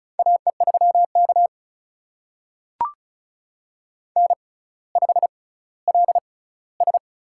Morse code, heard as a keyed tone, about 34 wpm, spaced as for 18 wpm, 700 Hz. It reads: AE3K N5LS
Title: USA Call Signs - 4 Characters - 6x Character Spacing - 35wpm